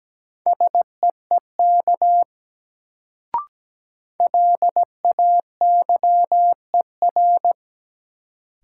Morse code, heard as a keyed tone, about 17 wpm, 700 Hz. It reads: SEEK LAYER
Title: Sentences from Top 1600 Words - New Word Drill - 17wpm